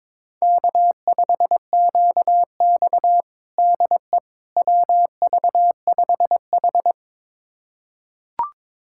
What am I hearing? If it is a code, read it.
Morse code, heard as a keyed tone, about 21 wpm, 700 Hz. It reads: K5QX DE WV5H
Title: USA Callsign DE Callsign - 4 Characters - 22wpm